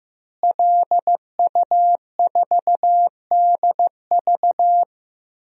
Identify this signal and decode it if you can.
Morse code, heard as a keyed tone, about 16 wpm, 700 Hz. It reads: LU4DV
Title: DXCC-100 Call Signs - 5 Characters - Repeated 2x - 15wpm